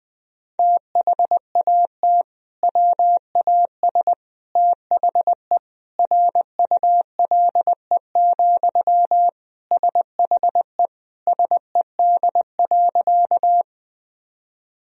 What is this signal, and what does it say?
Morse code, heard as a keyed tone, about 20 wpm, 700 Hz. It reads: THAT WAS THE RULE, SHE SED.
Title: Sentences from Top 400 Words and Abbreviations 20wpm